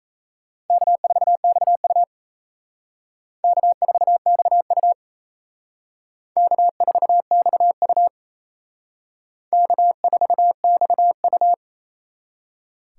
Morse code, heard as a keyed tone, about 35 wpm, 700 Hz. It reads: K4XU K4XU K4XU K4XU
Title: Speed Racing - USA Call Signs 28wpm